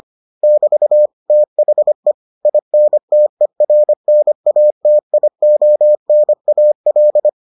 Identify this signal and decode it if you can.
Morse code, heard as a keyed tone, about 25 wpm, 600 Hz. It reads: <BT> THE INTERNATIONAL